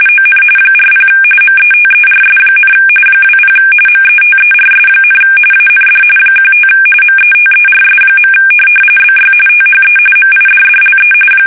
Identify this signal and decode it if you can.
RTTY, about 45.45 baud. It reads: ON 40 METERS, 20 METERS, AND 15 METERS. BY THE MIDDLE
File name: fsk850 / teletype-RTTY